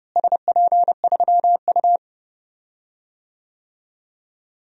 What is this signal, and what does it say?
Morse code, heard as a keyed tone, about 30 wpm, 700 Hz. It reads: SP3U
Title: DXCC-100 Call Signs - 4 Characters - Repeated 3x - 30wpm